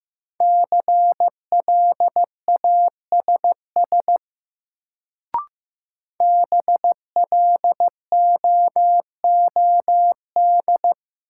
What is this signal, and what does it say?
Morse code, heard as a keyed tone, about 15 wpm, 700 Hz. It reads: CLASS BLOOD
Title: Sentences from Top 1000 Words - Review Part 2 - New Word Drill - 15wpm